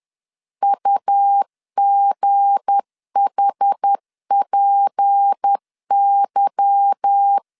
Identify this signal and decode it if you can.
Morse code, heard as a keyed tone, about 10 wpm, 800 Hz. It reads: UGHPY
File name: CW / DET / 1 9 1500C S B 10wpm 4439